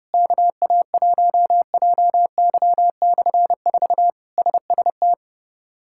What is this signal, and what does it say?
Morse code, heard as a keyed tone, about 30 wpm, 700 Hz. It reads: KA1JY/4 SST
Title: K1USN SST QSO Practice - 30wpm